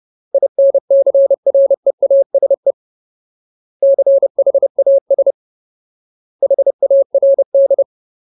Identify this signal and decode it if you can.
Morse code, heard as a keyed tone, about 30 wpm, 550 Hz. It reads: INCREASE CHAS HARD
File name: Ultimate Word List / 30WPM Ultimate